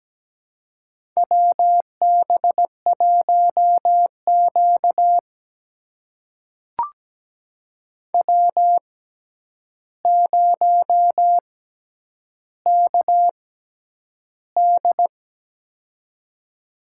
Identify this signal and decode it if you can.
Morse code, heard as a keyed tone, about 17 wpm, spaced as for 9 wpm, 700 Hz. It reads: WB1Q W0KD
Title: USA Call Signs - 6x Character Spacing - 17wpm